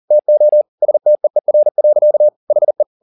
Morse code, heard as a keyed tone, about 40 wpm, 600 Hz. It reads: TO STEER. HE
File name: the old man and the sea 2 40wpm